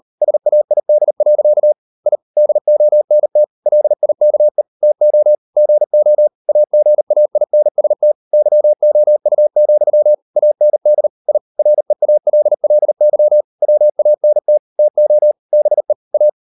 Morse code, heard as a keyed tone, about 38 wpm, 600 Hz. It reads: SAID. I DONT LIKE TO GO AGAINST YOU, AND I REALLY WANT TO BE A